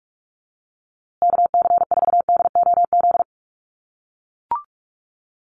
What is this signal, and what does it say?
Morse code, heard as a keyed tone, about 44 wpm, 700 Hz. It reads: KC4DOZ
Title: USA Call Signs Repeated 3x 45wpm